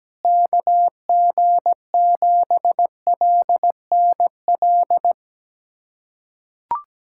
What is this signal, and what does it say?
Morse code, heard as a keyed tone, about 17 wpm, 700 Hz. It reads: KG7LNL
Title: USA Callsign DE Callsign 17wpm